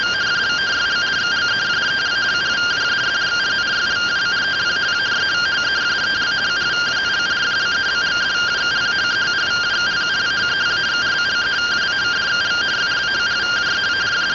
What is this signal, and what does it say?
Two-tone teleprinter signal, about 170 Hz shift.